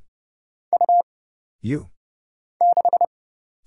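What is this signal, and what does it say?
Morse code, heard as a keyed tone, about 30 wpm, 700 Hz. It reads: U 6
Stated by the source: Morse Code Ninja single-character practice (letters, numbers, punctuation, prosigns) at 30 wpm, by Kurt Zoglmann (AD0WE).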